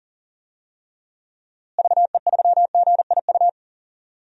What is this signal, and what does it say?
Morse code, heard as a keyed tone, about 40 wpm, 700 Hz. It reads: VE3GIU
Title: DXCC-100 Call Signs - 6 Characters - Repeated 3x - 40wpm